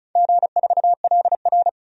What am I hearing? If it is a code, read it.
Morse code, heard as a keyed tone, about 35 wpm, 690 Hz. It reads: G4LR
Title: DXCC-100 Call Signs - 4 Characters - 35wpm